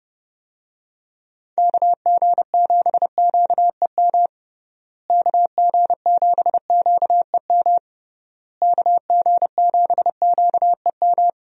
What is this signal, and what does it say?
Morse code, heard as a keyed tone, about 31 wpm, 700 Hz. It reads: KG7QEM KG7QEM KG7QEM
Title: USA Call Signs - 6 Characters - Repeated 3x - 30wpm